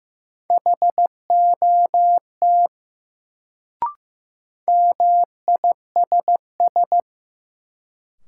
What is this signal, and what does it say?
Morse code, heard as a keyed tone, about 15 wpm, 700 Hz. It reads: HOT MISS
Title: Sentences from Top 500 Words - New Word Drill - 15wpm